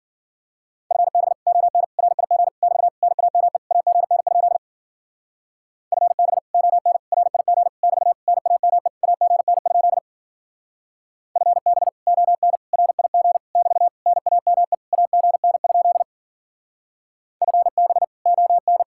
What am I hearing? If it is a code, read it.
Morse code, heard as a keyed tone, about 51 wpm, 700 Hz. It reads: FB ON RIG <BT> NAME AGN? FB ON RIG <BT> NAME AGN? FB ON RIG <BT> NAME AGN? FB ON